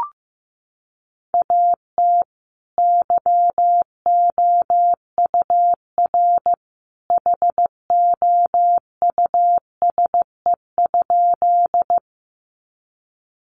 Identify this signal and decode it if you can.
Morse code, heard as a keyed tone, about 15 wpm, 700 Hz. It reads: AT YOUR HOUSE?
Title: Sentences from Top 200 Words 15wpm